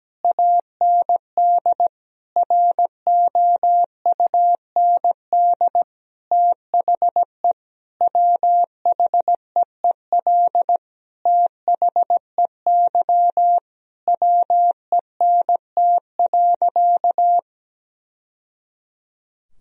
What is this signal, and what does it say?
Morse code, heard as a keyed tone, about 17 wpm, 700 Hz. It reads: AND ROUND THE WHEEL THEY WENT.
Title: Sentences from Top 1000 Words - Review Part 3 - 17wpm